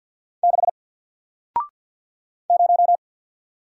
Morse code, heard as a keyed tone, about 53 wpm, 700 Hz. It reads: 6 0
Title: Single Letter-Number 50wpm